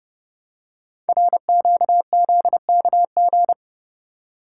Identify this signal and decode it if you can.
Morse code, heard as a keyed tone, about 28 wpm, 700 Hz. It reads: RQZKG
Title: Code Groups - 5 Letter:Numbers - 30wpm